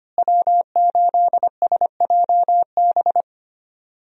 Morse code, heard as a keyed tone, about 25 wpm, 700 Hz. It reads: W8SJB
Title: USA Call Signs - 5 Characters - Repeated 3x - 25wpm